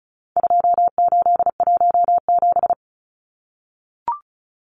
Morse code, heard as a keyed tone, about 34 wpm, 700 Hz. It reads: 2817
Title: Numbers - 4-Digit 35wpm